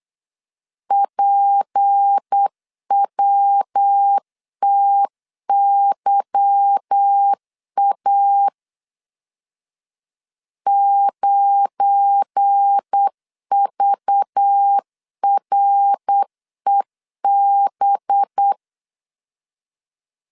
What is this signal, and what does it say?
Morse code, heard as a keyed tone, about 8 wpm, 800 Hz. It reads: PWTYA 9VREB
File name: LEZIONI / 9 2 8wpm 3437